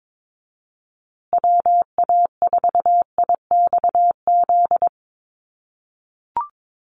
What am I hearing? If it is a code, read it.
Morse code, heard as a keyed tone, about 22 wpm, 700 Hz. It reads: WA4IXZ
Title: USA Call Signs Repeated 3x 22wpm